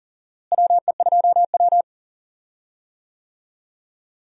Morse code, heard as a keyed tone, about 40 wpm, 700 Hz. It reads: WE2W
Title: USA Call Signs - 4 Characters - Repeated 2x - 40wpm